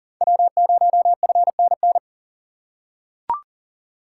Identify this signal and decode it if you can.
Morse code, heard as a keyed tone, about 40 wpm, 700 Hz. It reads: W0FNN
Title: USA Call Signs - 5 Characters - 6x Character Spacing - 40wpm